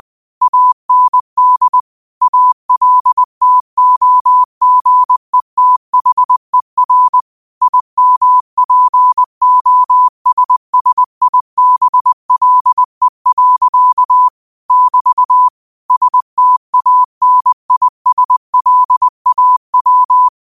Morse code, heard as a keyed tone, about 20 wpm, 1000 Hz. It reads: AND ALTOGETHER IMPOSSIBLE. <BT> STANISLAW
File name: qotdfiles / 20wpm